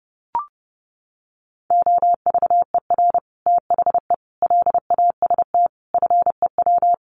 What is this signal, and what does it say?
Morse code, heard as a keyed tone, about 30 wpm, 700 Hz. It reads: OVER THE LAST FEW
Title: Sets of 5 Words Encore 30wpm